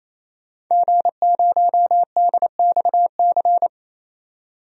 Morse code, heard as a keyed tone, about 28 wpm, 700 Hz. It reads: G0DXC
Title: DXCC-100 Call Signs - 5 Characters - 28wpm